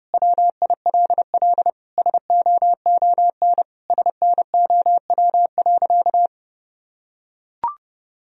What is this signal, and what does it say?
Morse code, heard as a keyed tone, about 30 wpm, 700 Hz. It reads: WILL SOON SNOW.
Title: Sentences from Top 1000 Words - Review Part 3 - 30wpm